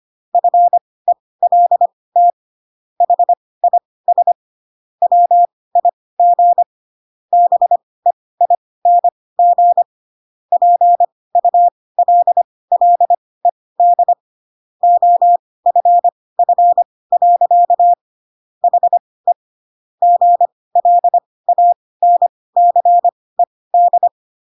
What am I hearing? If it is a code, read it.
Morse code, heard as a keyed tone, about 25 wpm, spaced as for 18 wpm, 700 Hz. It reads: FELT HIS WIG BEING PULLED OFF. HE GLANCED
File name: Pinocchio ch3 12 25 ik1hge